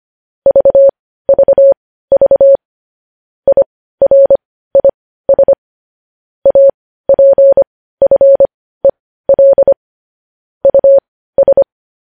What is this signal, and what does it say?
Morse code, heard as a keyed tone, about 25 wpm, spaced as for 15 wpm, 550 Hz. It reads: VVV IRIS APFEL US
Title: News Headlines at 15